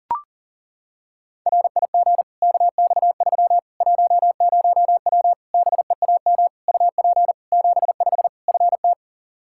Morse code, heard as a keyed tone, about 40 wpm, 700 Hz. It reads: RIG KX3 10W BEAM UP 75 FT
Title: Practice QSO - Exchange 2 - Info - 40wpm